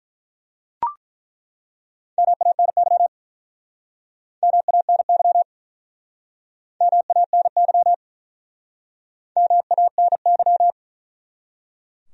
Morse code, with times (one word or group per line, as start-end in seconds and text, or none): 2.2-3.1 MANY
4.4-5.4 MANY
6.8-7.9 MANY
9.4-10.7 MANY